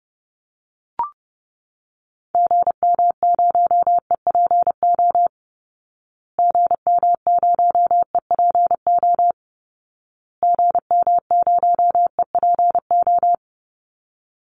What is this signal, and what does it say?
Morse code, heard as a keyed tone, about 30 wpm, 700 Hz. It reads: GM0EPO GM0EPO GM0EPO
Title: Call Signs Repeated 2x - 30wpm